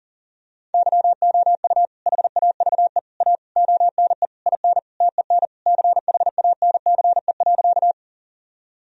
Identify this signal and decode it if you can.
Morse code, heard as a keyed tone, about 40 wpm, 700 Hz. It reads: YOU HAVE A ONE IN TEN CHANCE.
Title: Sentences from Top 1000 Words - Review Part 2 - 40wpm